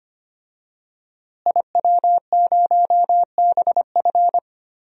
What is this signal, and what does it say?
Morse code, heard as a keyed tone, about 25 wpm, 700 Hz. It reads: IW0BF
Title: DXCC-100 Call Signs - 5 Characters - Repeated 2x - 25wpm